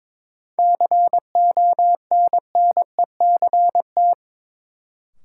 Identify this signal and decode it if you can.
Morse code, heard as a keyed tone, about 22 wpm, 700 Hz. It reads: CONNECT
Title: Sentences from Top 1500 Words - New Word Drill Alt - 22wpm